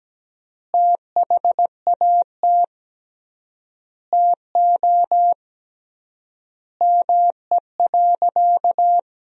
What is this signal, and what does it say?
Morse code, heard as a keyed tone, about 17 wpm, 700 Hz. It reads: THAT TO ME.
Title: 3x Word-Spacing - Sentences from Top 400 Words - 17wpm